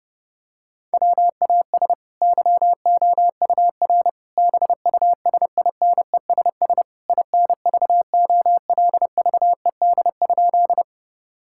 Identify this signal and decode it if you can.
Morse code, heard as a keyed tone, about 30 wpm, 700 Hz. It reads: WAS YOUR BUSINESS INVOLVED?
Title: Sentences from Top 1000 Words - Review Part 5 - 30wpm